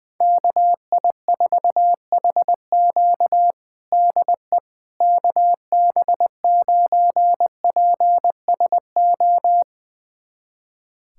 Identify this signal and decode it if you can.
Morse code, heard as a keyed tone, about 20 wpm, 700 Hz. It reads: KI4HQ DE KB9PSO
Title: USA Callsign DE Callsign 20wpm